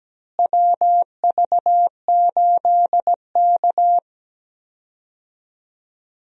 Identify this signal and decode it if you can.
Morse code, heard as a keyed tone, about 17 wpm, 700 Hz. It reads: WV8K
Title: USA Call Signs - 4 Characters - Repeated 3x - 17wpm